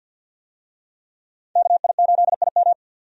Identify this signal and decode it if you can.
Morse code, heard as a keyed tone, about 51 wpm, 700 Hz. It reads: KI8IM
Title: USA Call Signs - 5 Characters - 50wpm